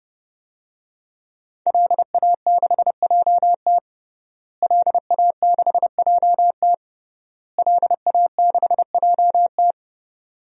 Morse code, heard as a keyed tone, about 30 wpm, 700 Hz. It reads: LA6JT LA6JT LA6JT
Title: DXCC-100 Call Signs - 5 Characters - Repeated 3x - 30wpm